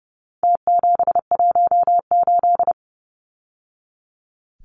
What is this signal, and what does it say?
Morse code, heard as a keyed tone, about 30 wpm, 700 Hz. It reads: T718
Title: Numbers - Contest Style - 4-Digit 30wpm